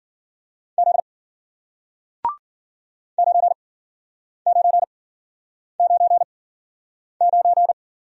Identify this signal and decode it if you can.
Morse code, heard as a keyed tone, about 53 wpm, spaced as for 9 wpm, 700 Hz. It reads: Z 9999